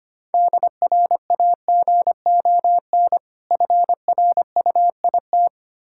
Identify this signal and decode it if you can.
Morse code, heard as a keyed tone, about 25 wpm, 700 Hz. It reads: DRAGON FRUIT
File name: practice / 25wpm / Difficult Words with Hints - 25wpm